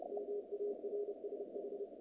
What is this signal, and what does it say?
Morse code, heard as a keyed tone, about 30 wpm, 440 Hz, with noise. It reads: MARSH
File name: contests / naqp-cw-k5tr-op-k5pi- -2200